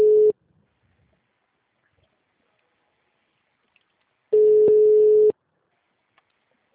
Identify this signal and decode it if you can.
Ringback tone (Europe, 425 Hz).
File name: out- -2021- - - .12347